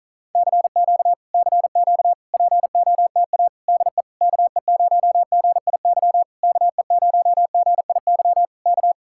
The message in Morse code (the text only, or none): CQ CQ POTA DE KE0GIY KE0GIY K